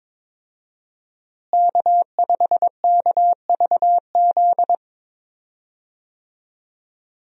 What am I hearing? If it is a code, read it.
Morse code, heard as a keyed tone, about 22 wpm, 700 Hz. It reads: K5KVZ
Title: USA Call Signs - 5 Characters - Repeated 3x - 22wpm